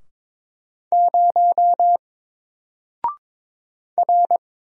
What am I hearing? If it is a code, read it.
Morse code, heard as a keyed tone, about 22 wpm, 700 Hz. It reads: 0 R
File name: Single-Letter-Number-22wpm